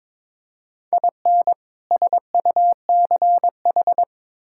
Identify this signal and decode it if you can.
Morse code, heard as a keyed tone, about 22 wpm, 700 Hz. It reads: IN SUCH
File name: practice / 22wpm / Build A Phrase - 5-Word Phrases - 22wpm